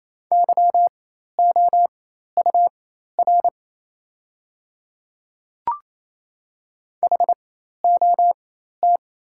Morse code, heard as a keyed tone, about 28 wpm, spaced as for 13 wpm, 700 Hz. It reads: YOUR HOT
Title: 4x Character Spacing - Top 100 Words - 28wpm